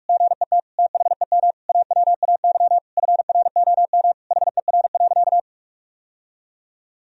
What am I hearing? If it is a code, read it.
Morse code, heard as a keyed tone, about 46 wpm, 700 Hz. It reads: GET THEM AWAY FROM HER.
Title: Sentences from Top 1000 Words - Review Part 2 - 45wpm